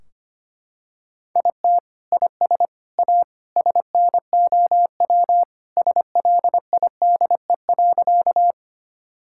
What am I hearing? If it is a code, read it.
Morse code, heard as a keyed tone, about 25 wpm, 700 Hz. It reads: IT IS A SNOW SLIDE.
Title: Sentences from Top 1600 Words 25wpm